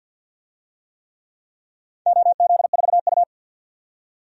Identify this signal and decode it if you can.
Morse code, heard as a keyed tone, about 52 wpm, 700 Hz. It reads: OZ4U
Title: DXCC-100 Call Signs - 4 Characters - 50wpm